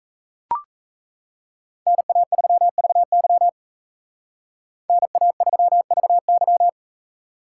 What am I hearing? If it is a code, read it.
Morse code, heard as a keyed tone, about 36 wpm, 700 Hz. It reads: NA3VY NA3VY